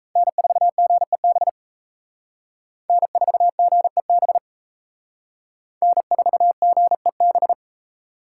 Morse code, heard as a keyed tone, about 34 wpm, 700 Hz. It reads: N4GEB N4GEB N4GEB
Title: Speed Racing - USA Call Signs 28wpm